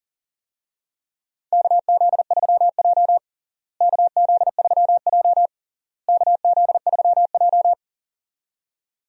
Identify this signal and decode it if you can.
Morse code, heard as a keyed tone, about 37 wpm, 700 Hz. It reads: KZ3J KZ3J KZ3J
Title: USA Call Signs - 4 Characters - Repeated 3x - 40wpm